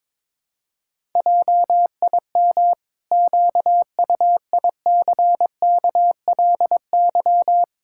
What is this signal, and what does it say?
Morse code, heard as a keyed tone, about 22 wpm, 700 Hz. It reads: JIM QUICKLY
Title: Pangram Challenge v2 - 22wpm